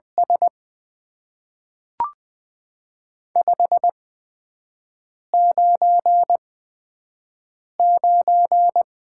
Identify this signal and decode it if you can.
Morse code, heard as a keyed tone, about 20 wpm, spaced as for 5 wpm, 700 Hz. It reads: S 599
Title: Top 90 QSO Elements - 8x Character Spacing - 20wpm